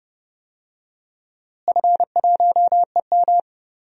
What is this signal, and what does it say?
Morse code, heard as a keyed tone, about 31 wpm, 700 Hz. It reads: F1EM
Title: DXCC-100 Call Signs - 4 Characters - Repeated 3x - 30wpm